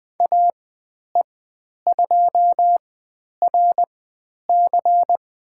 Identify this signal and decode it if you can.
Morse code, heard as a keyed tone, about 20 wpm, spaced as for 10 wpm, 700 Hz. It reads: AE2RC
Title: 095 Lesson 16g - Letter c - Call Signs 20wpm